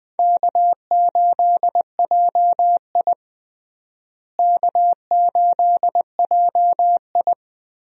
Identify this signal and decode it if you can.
Morse code, heard as a keyed tone, about 21 wpm, 700 Hz. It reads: K8JI K8JI
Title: USA Call Signs - 4 Characters - Repeated 3x - 20wpm